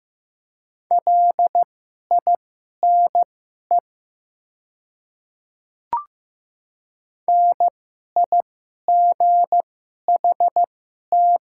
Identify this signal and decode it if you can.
Morse code, heard as a keyed tone, about 15 wpm, spaced as for 11 wpm, 700 Hz. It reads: LINE NIGHT
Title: 2x Character Spacing - Top 300 Words - 15wpm